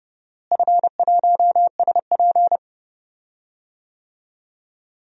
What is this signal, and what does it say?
Morse code, heard as a keyed tone, about 29 wpm, 700 Hz. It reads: F1SP
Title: DXCC-100 Call Signs - 4 Characters - Repeated 2x - 30wpm